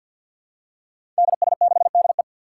Morse code, heard as a keyed tone, about 49 wpm, 700 Hz. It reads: DS6DE
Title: Five Challenging Letter-Number Combinations 50wpm